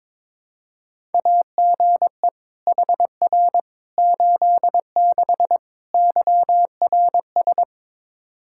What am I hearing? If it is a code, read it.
Morse code, heard as a keyed tone, about 22 wpm, 700 Hz. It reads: AGE HR 86 YRS
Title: Practice QSO - Exchange 3 - More Info - 22wpm